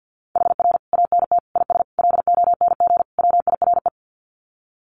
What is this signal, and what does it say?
Morse code, heard as a keyed tone, about 48 wpm, 700 Hz. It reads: HR ANT IS LONG WIRE
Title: Common QSO Phrases 50wpm